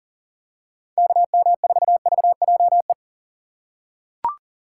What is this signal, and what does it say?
Morse code, heard as a keyed tone, about 40 wpm, 700 Hz. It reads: KM4VJE
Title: USA Call Signs - 6 Characters - 3x Character Spacing - 40wpm